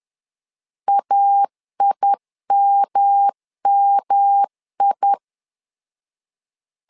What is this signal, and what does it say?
Morse code, heard as a keyed tone, about 11 wpm, 800 Hz. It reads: AIMMI
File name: LEZIONI / 3 MWRO / 3 M 10wpm 1828